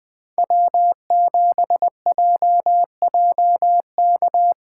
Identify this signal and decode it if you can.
Morse code, heard as a keyed tone, about 20 wpm, 700 Hz. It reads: W7JJK
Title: Speed Racing - USA 1x3 Call Signs 17wpm